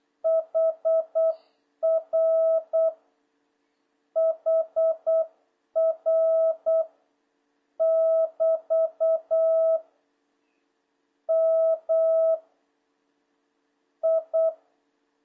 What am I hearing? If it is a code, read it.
Morse code, heard as a keyed tone, about 8 wpm, 650 Hz. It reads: HR HR <BT> M I
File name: jh8cbh / 2026cw